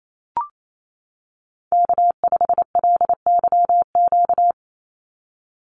Morse code, heard as a keyed tone, about 28 wpm, 700 Hz. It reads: K5LYQ
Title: USA Call Signs 28wpm